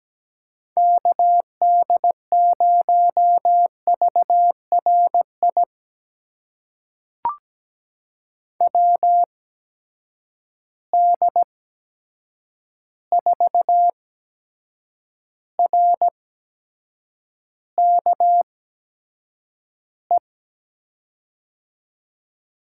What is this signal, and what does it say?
Morse code, heard as a keyed tone, about 17 wpm, spaced as for 7 wpm, 700 Hz. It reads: KD0VRI WD4RKE
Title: USA Call Signs - 6 Characters - 8x Character Spacing - 17wpm